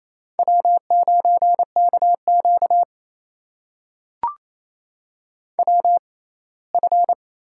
Morse code, heard as a keyed tone, about 28 wpm, spaced as for 19 wpm, 700 Hz. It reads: W9KQ WF
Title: USA Call Signs - 4 Characters - 6x Character Spacing - 28wpm